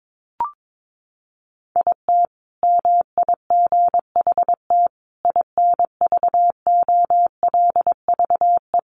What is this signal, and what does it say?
Morse code, heard as a keyed tone, about 22 wpm, 700 Hz. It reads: IT MIGHT INVOLVE
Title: Sentences from Top 1200 Words 22wpm